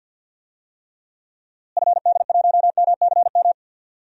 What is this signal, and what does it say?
Morse code, heard as a keyed tone, about 50 wpm, 700 Hz. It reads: FD1MCM
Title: DXCC-100 Call Signs - 6 Characters - Repeated 3x - 50wpm